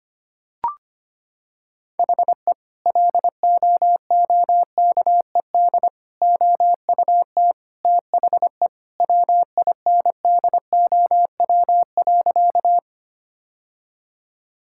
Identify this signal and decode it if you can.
Morse code, heard as a keyed tone, about 25 wpm, 700 Hz. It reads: HE LOOKED OUT THE WINDOW.
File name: practice / 25wpm / Sentences with 2-Word Phrases - 25wpm